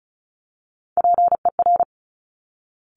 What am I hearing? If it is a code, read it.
Morse code, heard as a keyed tone, about 36 wpm, 700 Hz. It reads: PER
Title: Top Three Letter Combinations in Words 35wpm